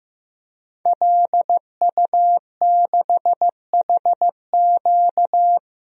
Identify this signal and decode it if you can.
Morse code, heard as a keyed tone, about 15 wpm, 700 Hz. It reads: LU6HQ
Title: DXCC-100 Call Signs - 5 Characters - Repeated 2x - 15wpm